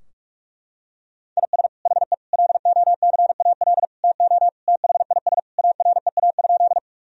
Morse code, heard as a keyed tone, about 46 wpm, 700 Hz. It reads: IS HE LOCAL TO THIS AREA?
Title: Sentences from Top 1000 Words - Review Part 3 - 45wpm